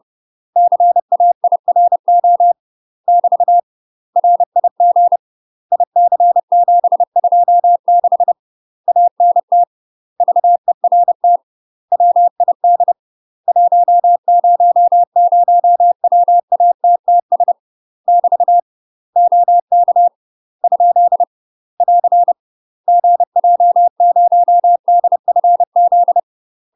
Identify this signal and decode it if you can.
Morse code, heard as a keyed tone, about 30 wpm, 700 Hz. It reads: CAIRO <BT> RIG IC726 ANT VERT WID 100WATTS <BT> OK ? <AR> GJ0DFZ
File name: UK MASTER QSOs TRAINING / 30WPM / UK MASTER QSOs 0003